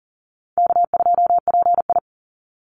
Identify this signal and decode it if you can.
Morse code, heard as a keyed tone, about 42 wpm, 700 Hz. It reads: K2PI
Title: Call Signs - 40wpm